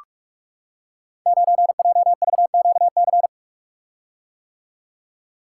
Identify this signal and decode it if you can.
Morse code, heard as a keyed tone, about 44 wpm, 700 Hz. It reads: 9JVQC
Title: Code Groups - 5 Letter:Numbers - 45wpm